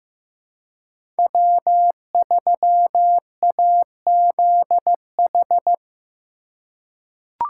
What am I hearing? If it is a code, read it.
Morse code, heard as a keyed tone, about 15 wpm, 700 Hz. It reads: W3AZH
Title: USA Call Signs - 5 Characters - 2x Character Spacing - 15wpm